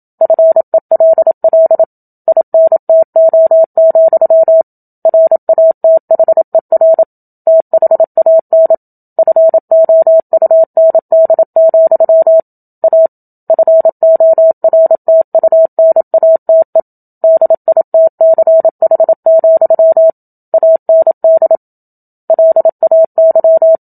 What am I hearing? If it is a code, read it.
Morse code, heard as a keyed tone, about 27 wpm, 650 Hz. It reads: FELL INTO, RATHER THAN FOUND, A FORTUNATE DITCH, AND LAY
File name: Book Two / 01-Chapter 1 - 27 WPM